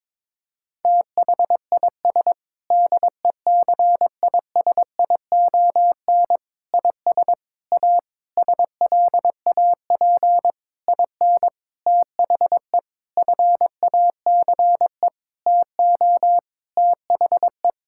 Morse code, heard as a keyed tone, about 22 wpm, 700 Hz. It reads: THIS DECISION IS A SLAP IN THE FACE TO THE
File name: practice / 22wpm / Sentences with 8-Word Phrases - 22wpm